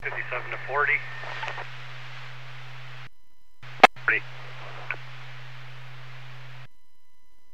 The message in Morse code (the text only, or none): none